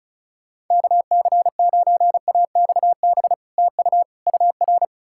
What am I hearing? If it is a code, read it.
Morse code, heard as a keyed tone, about 48 wpm, spaced as for 42 wpm, 700 Hz. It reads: KC9AXB TU UR
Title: Speed-Racing Common QSO Phrases - 35wpm